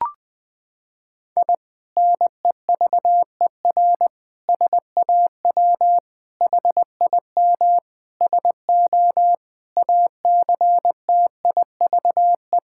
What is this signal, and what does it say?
Morse code, heard as a keyed tone, about 20 wpm, 700 Hz. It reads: I NEVER SAW HIM SO ACTIVE
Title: Sentences from Top 1000 Words - Review Part 6 - 20wpm